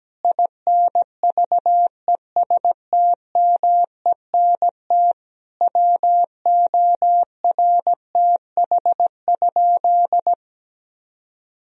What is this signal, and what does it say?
Morse code, heard as a keyed tone, about 17 wpm, 700 Hz. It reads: INVESTMENT WORTH?
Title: Sentences from Top 1000 Words - Review Part 4 - 17wpm